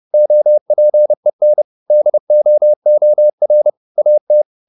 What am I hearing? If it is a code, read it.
Morse code, heard as a keyed tone, about 30 wpm, 600 Hz. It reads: OPEN DOOR AT
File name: the old man and the sea / the old man and the sea 1 30wpm